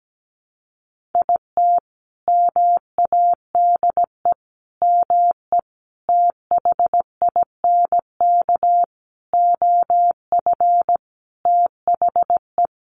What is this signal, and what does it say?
Morse code, heard as a keyed tone, about 17 wpm, 700 Hz. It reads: IT MADE ME THINK OF THE
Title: Sentences from Top 1100 Words 17wpm